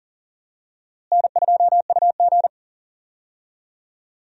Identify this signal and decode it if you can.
Morse code, heard as a keyed tone, about 40 wpm, 700 Hz. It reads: N2UG
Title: USA Call Signs - 4 Characters - Repeated 2x - 40wpm